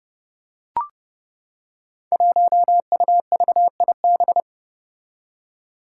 Morse code, heard as a keyed tone, about 29 wpm, 700 Hz. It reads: 1UVIB
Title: Five Challenging Letter-Number Combinations 30wpm